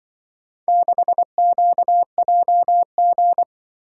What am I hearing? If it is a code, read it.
Morse code, heard as a keyed tone, about 25 wpm, 700 Hz. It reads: 6QJG
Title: Speed-Racing - Four Letter-Number Combinations - 20wpm